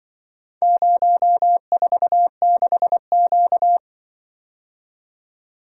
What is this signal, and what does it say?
Morse code, heard as a keyed tone, about 25 wpm, 700 Hz. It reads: 046Q
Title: Speed-Racing - Four Letter-Number Combinations - 20wpm